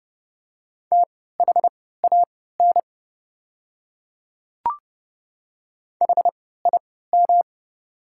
Morse code, heard as a keyed tone, about 31 wpm, spaced as for 17 wpm, 690 Hz. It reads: THAN HIM
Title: 3x Character Spacing - Top 100 Words - 30wpm